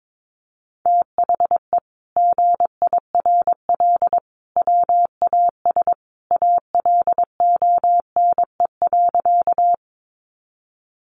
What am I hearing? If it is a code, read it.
Morse code, heard as a keyed tone, about 22 wpm, 700 Hz. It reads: THE GIRL WAS ALONE.
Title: Sentences from Top 1000 Words 22wpm